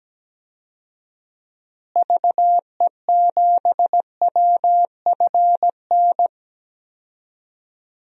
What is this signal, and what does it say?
Morse code, heard as a keyed tone, about 17 wpm, 700 Hz. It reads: VE7WFN
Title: DXCC-100 Call Signs - 6 Characters - Repeated 2x - 17wpm